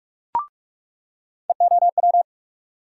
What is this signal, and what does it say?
Morse code, heard as a keyed tone, about 48 wpm, 700 Hz. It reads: EOW
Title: USA Call Sign Postfixes - 3 Letters 45wpm